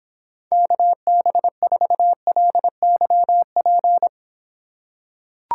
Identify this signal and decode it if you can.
Morse code, heard as a keyed tone, about 26 wpm, 700 Hz. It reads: KB4LYP